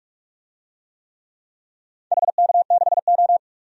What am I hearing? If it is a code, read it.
Morse code, heard as a keyed tone, about 43 wpm, 700 Hz. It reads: HK6O
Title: DXCC-100 Call Signs - 4 Characters - 45wpm